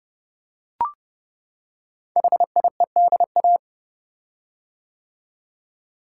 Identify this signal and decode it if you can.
Morse code, heard as a keyed tone, about 30 wpm, 700 Hz. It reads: HIEDA
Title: Code Groups - 5 Letters - 30wpm